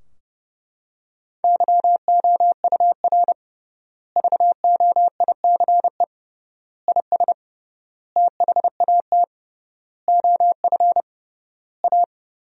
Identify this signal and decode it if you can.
Morse code, heard as a keyed tone, about 28 wpm, 700 Hz. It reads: YOUR VOICE IS THAT OF A
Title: 3x Word-Spacing - Sentences from Top 500 Words - 30wpm